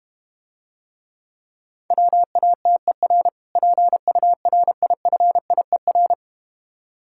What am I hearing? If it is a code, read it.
Morse code, heard as a keyed tone, about 31 wpm, 700 Hz. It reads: WATER PURIFIER
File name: Easy Words - Speed-Racing with Hints - 25wpm